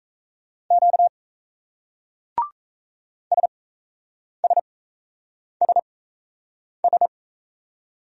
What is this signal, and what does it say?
Morse code, heard as a keyed tone, about 38 wpm, spaced as for 8 wpm, 700 Hz. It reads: Q SSSS